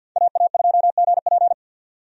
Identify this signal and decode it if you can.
Morse code, heard as a keyed tone, about 53 wpm, 700 Hz. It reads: AA2GP
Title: USA Call Signs - 5 Characters - 50wpm